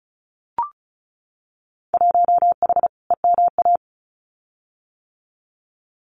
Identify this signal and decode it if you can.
Morse code, heard as a keyed tone, about 36 wpm, 700 Hz. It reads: 1H EMA
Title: Winter Field Day Exchanges 35wpm